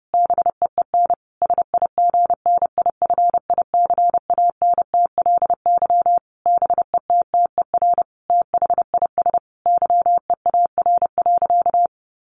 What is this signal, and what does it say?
Morse code, heard as a keyed tone, about 30 wpm, 700 Hz. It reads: BEEN SIGNIFICANTLY BETTER THIS YEAR.
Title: Sentences from Top 1500 Words 30wpm